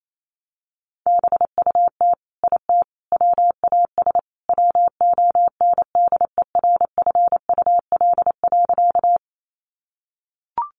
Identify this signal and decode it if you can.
Morse code, heard as a keyed tone, about 28 wpm, 700 Hz. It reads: BUT IT WAS WONDERFUL.
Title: Sentences from Top 1100 Words 28wpm